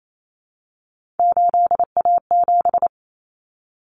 Morse code, heard as a keyed tone, about 28 wpm, 700 Hz. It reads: 8A7
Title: Common Cut Numbers - 3-Digit 28wpm